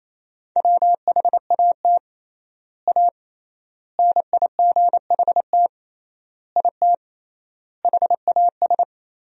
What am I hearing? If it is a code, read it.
Morse code, heard as a keyed tone, about 28 wpm, 700 Hz. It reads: WHAT A NIGHT IT HAS
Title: 3x Word-Spacing - Sentences from Top 300 Words - 28wpm